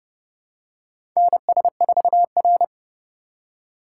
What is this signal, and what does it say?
Morse code, heard as a keyed tone, about 30 wpm, 700 Hz. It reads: NS4R
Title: USA Call Signs - 4x Character Spacing - 30wpm